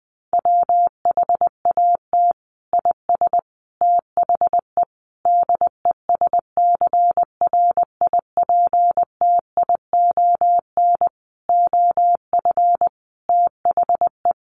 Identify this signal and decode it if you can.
Morse code, heard as a keyed tone, about 20 wpm, 700 Hz. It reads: WHAT IS THE DESCRIPTION OF THE
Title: Sentences from Top 1300 Words 20wpm